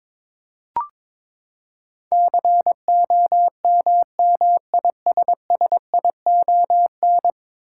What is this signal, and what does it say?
Morse code, heard as a keyed tone, about 22 wpm, 700 Hz. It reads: COMMISSION
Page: morse-code-ninja-practice-22wpm